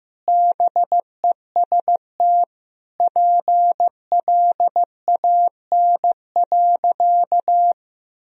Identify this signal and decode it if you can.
Morse code, heard as a keyed tone, about 15 wpm, 700 Hz. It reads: BEST PLAN.
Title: Sentences from Top 1000 Words - Review Part 2 - 15wpm